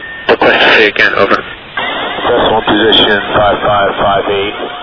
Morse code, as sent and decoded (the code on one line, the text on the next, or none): none